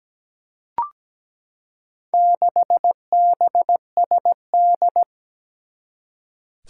Morse code, heard as a keyed tone, about 17 wpm, 700 Hz. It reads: 6BSD
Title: Four Challenging Letter-Number Combinations 17wpm